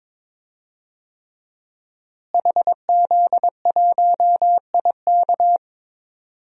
Morse code, heard as a keyed tone, about 22 wpm, 700 Hz. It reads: HZ1IK
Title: DXCC-100 Call Signs - 5 Characters - Repeated 3x - 22wpm